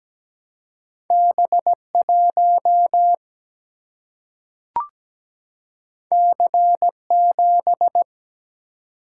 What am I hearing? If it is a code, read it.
Morse code, heard as a keyed tone, about 17 wpm, 700 Hz. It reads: B1 C7
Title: Single-Letter Single-Number 17wpm